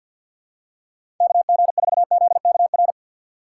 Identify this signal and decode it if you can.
Morse code, heard as a keyed tone, about 52 wpm, 700 Hz. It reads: KG4ZKR
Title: USA Call Signs - 6x Character Spacing - 50wpm